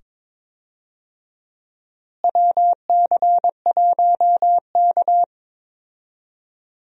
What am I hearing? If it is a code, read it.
Morse code, heard as a keyed tone, about 22 wpm, 700 Hz. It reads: WC1K
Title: USA Call Signs - 4 Characters - Repeated 2x - 22wpm